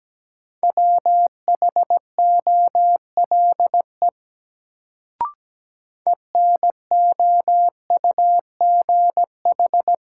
Morse code, heard as a keyed tone, about 17 wpm, 700 Hz. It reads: WHOLE ENOUGH
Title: Sentences from Top 400 Words - New Word Drill - 17wpm